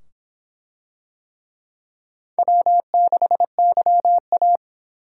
Morse code, heard as a keyed tone, about 25 wpm, 700 Hz. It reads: W6YA